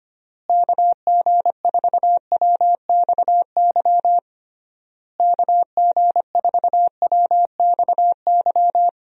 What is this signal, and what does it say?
Morse code, heard as a keyed tone, about 25 wpm, 700 Hz. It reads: KG4WXY KG4WXY
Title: USA Call Signs - 6 Characters - Repeated 3x - 25wpm